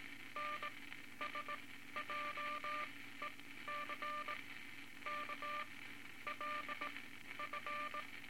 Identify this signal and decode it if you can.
Morse code, heard as a keyed tone, about 18 wpm, spaced as for 12 wpm, 1190 Hz, with noise. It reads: NSJEC K LF
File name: Morse Code Records / 18 A LCM 164 Ten Words Per Minute